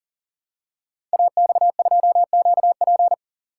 Morse code, heard as a keyed tone, about 39 wpm, 700 Hz. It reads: AX2QP
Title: Code Groups - 5 Letter:Numbers - 40wpm